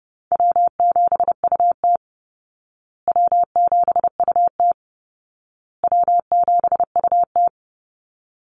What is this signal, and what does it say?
Morse code, heard as a keyed tone, about 30 wpm, 700 Hz. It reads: W7UT W7UT W7UT
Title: Easy Call Signs Repeated 3x - 30wpm